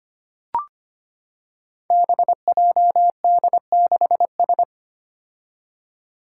Morse code, heard as a keyed tone, about 25 wpm, 700 Hz. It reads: BJD6S
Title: Five Challenging Letter-Number Combinations 25wpm